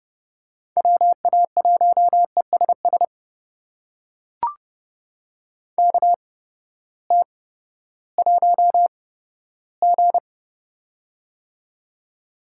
Morse code, heard as a keyed tone, about 31 wpm, spaced as for 15 wpm, 700 Hz. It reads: WA1ESS KT1G
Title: USA Call Signs - 8x Character Spacing - 30wpm